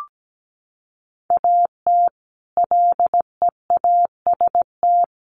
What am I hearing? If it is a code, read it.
Morse code, heard as a keyed tone, about 17 wpm, 700 Hz. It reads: AT LEAST
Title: Sentences from Top 1300 Words 17wpm